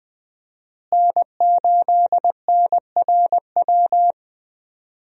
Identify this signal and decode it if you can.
Morse code, heard as a keyed tone, about 21 wpm, 700 Hz. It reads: N8NRW
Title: USA Call Signs - 5 Characters - Repeated 2x - 20wpm